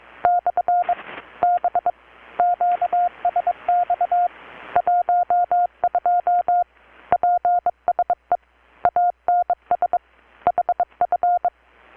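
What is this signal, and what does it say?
Morse code, heard as a keyed tone, about 23 wpm, 720 Hz. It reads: / B QSX 12 PSE ANS HF
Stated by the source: Morse from KPH's Night of Nights XX maritime radiotelegraphy event, received in Chiapas, Mexico about 3,800 km away on a Kenwood R-600 with a grounded 10 m long-wire antenna.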